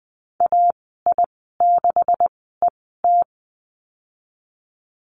Morse code, heard as a keyed tone, about 20 wpm, spaced as for 15 wpm, 700 Hz. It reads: AI6ET
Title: 173 Lesson 28g - Number 9 - Call Signs 20wpm